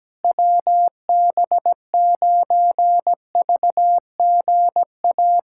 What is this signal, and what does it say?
Morse code, heard as a keyed tone, about 17 wpm, 700 Hz. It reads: WB9VGA
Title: USA Callsign DE Callsign - 6 Characters - 17wpm